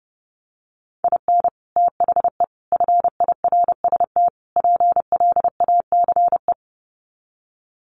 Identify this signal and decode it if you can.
Morse code, heard as a keyed tone, about 28 wpm, 700 Hz. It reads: IN THE FIRST PLACE
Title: Sets of 4 Words 30wpm